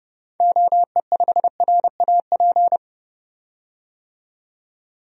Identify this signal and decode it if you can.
Morse code, heard as a keyed tone, about 30 wpm, 700 Hz. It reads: OE5RAP
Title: DXCC-100 Call Signs - 6 Characters - 30wpm